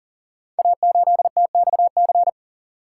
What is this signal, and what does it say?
Morse code, heard as a keyed tone, about 38 wpm, 690 Hz. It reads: A8TXC
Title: Code Groups - 5 Letter:Numbers - 40wpm